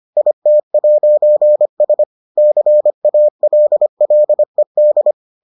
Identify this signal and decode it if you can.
Morse code, heard as a keyed tone, about 25 wpm, 600 Hz. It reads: IT'S CALLED